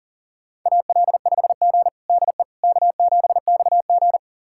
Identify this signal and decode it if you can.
Morse code, heard as a keyed tone, about 40 wpm, 700 Hz. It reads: AL5G DE K7XG
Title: USA Callsign DE Callsign - 4 Characters - 40wpm